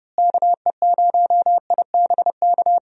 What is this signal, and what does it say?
Morse code, heard as a keyed tone, about 31 wpm, 700 Hz. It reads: KE0IBK
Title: USA Call Signs - 6 Characters - Repeated 2x - 30wpm